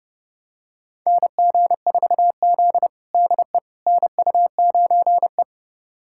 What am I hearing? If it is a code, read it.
Morse code, heard as a keyed tone, about 30 wpm, 700 Hz. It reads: NG4Z DE NU9E
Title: USA Callsign DE Callsign - 4 Characters - 30wpm